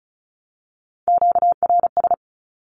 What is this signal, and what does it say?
Morse code, heard as a keyed tone, about 34 wpm, 700 Hz. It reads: QRS
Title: Q-Codes 35wpm